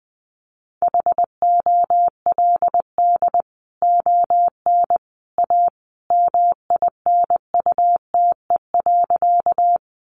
Morse code, heard as a keyed tone, about 20 wpm, 700 Hz. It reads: HOLD ON A MINUTE.
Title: Sentences from Top 500 Words 20wpm